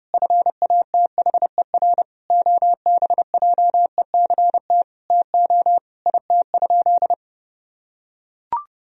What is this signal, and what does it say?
Morse code, heard as a keyed tone, about 30 wpm, 700 Hz. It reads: FATHER OBJECT TO IT?
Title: Sentences from Top 1000 Words - Review Part 3 - 30wpm